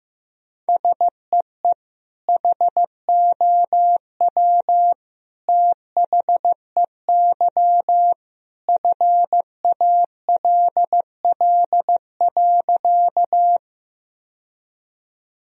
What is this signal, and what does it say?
Morse code, heard as a keyed tone, about 15 wpm, 700 Hz. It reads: SEE HOW THEY FALL.
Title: Sentences from Top 1000 Words - Review Part 2 - 15wpm